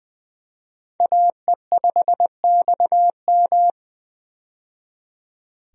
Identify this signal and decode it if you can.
Morse code, heard as a keyed tone, about 20 wpm, 700 Hz. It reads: AE5XM
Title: USA Call Signs - 5 Characters - 20wpm